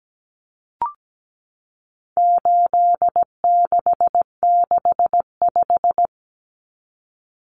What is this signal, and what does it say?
Morse code, heard as a keyed tone, about 17 wpm, 700 Hz. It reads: 8665
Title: Common Cut Numbers - 4-Digit 17wpm